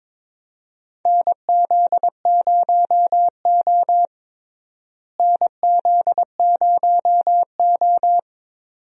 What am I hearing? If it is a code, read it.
Morse code, heard as a keyed tone, about 22 wpm, 700 Hz. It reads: NZ0O NZ0O
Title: USA Call Signs - 4 Characters - Repeated 2x - 22wpm